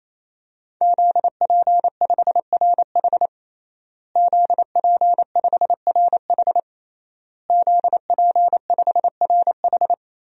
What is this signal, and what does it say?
Morse code, heard as a keyed tone, about 28 wpm, 700 Hz. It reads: ZP5RH ZP5RH ZP5RH
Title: DXCC-100 Call Signs - 5 Characters - Repeated 3x - 28wpm